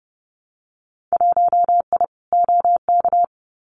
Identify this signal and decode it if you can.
Morse code, heard as a keyed tone, about 30 wpm, 700 Hz. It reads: 1I OK
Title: Winter Field Day Exchanges Encore 30wpm